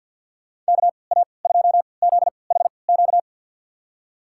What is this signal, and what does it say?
Morse code, heard as a keyed tone, about 49 wpm, spaced as for 28 wpm, 700 Hz. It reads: KA2ZHQ
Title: USA Call Signs - 3x Character Spacing - 50wpm